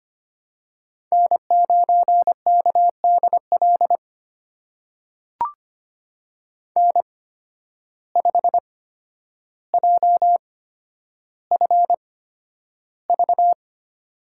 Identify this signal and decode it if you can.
Morse code, heard as a keyed tone, about 25 wpm, spaced as for 11 wpm, 700 Hz. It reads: N9KDL N5JFV
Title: USA Call Signs - 5 Characters - 8x Character Spacing - 25wpm